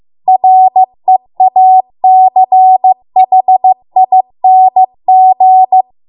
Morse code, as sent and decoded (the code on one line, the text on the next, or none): .-. .- -.-. .... .. -. --.
REACHING